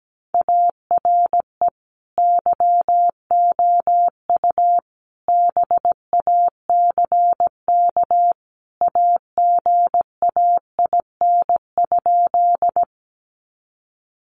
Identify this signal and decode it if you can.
Morse code, heard as a keyed tone, about 17 wpm, 700 Hz. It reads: ARE YOU BACK AGAIN?
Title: Sentences from Top 200 Words 17wpm